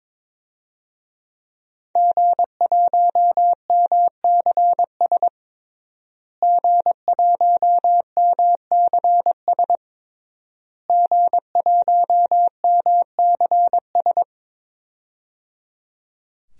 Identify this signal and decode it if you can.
Morse code, heard as a keyed tone, about 22 wpm, 700 Hz. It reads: G1MCS G1MCS G1MCS
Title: DXCC-100 Call Signs - 5 Characters - Repeated 3x - 22wpm